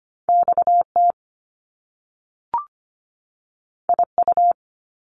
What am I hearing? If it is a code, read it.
Morse code, heard as a keyed tone, about 25 wpm, 700 Hz. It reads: XT IU
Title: Top Two Letter Combinations in Words 25wpm